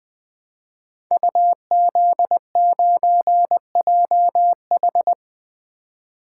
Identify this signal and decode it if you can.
Morse code, heard as a keyed tone, about 20 wpm, 700 Hz. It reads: UZ9JH
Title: Code Groups - 5 Letter:Numbers - 20wpm